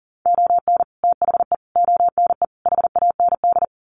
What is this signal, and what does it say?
Morse code, heard as a keyed tone, about 40 wpm, 700 Hz. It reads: ON THE ONE HAND
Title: Sets of 4 Words 40wpm